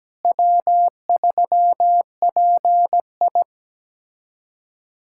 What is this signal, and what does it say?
Morse code, heard as a keyed tone, about 17 wpm, 700 Hz. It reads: W3PI
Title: USA Callsign DE Callsign - 4 Characters - 17wpm